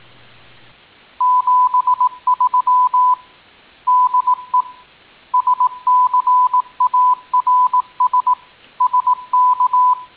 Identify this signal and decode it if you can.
Morse code, heard as a keyed tone, about 18 wpm, 1000 Hz. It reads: 73 DE SCARS SK